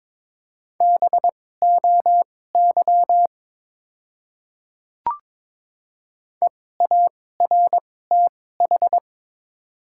Morse code, heard as a keyed tone, about 22 wpm, spaced as for 16 wpm, 700 Hz. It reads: BOY EARTH
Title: 2x Character Spacing - Top 200 Words - 22wpm